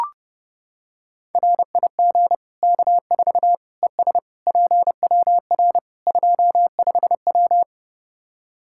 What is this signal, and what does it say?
Morse code, heard as a keyed tone, about 30 wpm, 700 Hz. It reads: RIG K4 ES PWR 25W
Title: Practice QSO - Exchange 2 - Info - 30wpm